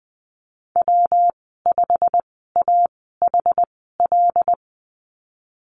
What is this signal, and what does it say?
Morse code, heard as a keyed tone, about 20 wpm, spaced as for 15 wpm, 700 Hz. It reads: W5AHL
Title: 173 Lesson 28g - Number 9 - Call Signs 20wpm